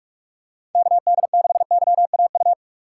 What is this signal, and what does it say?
Morse code, heard as a keyed tone, about 44 wpm, 700 Hz. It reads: KD6YAU
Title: USA Call Signs - 6x Character Spacing - 45wpm